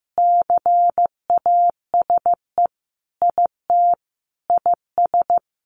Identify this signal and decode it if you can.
Morse code, heard as a keyed tone, about 15 wpm, 700 Hz. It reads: CASE IT IS
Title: Sentences from Top 1400 Words 15wpm